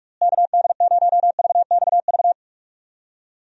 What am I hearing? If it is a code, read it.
Morse code, heard as a keyed tone, about 44 wpm, 700 Hz. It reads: KD0VXV
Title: USA Call Signs - 6 Characters - Repeated 3x - 45wpm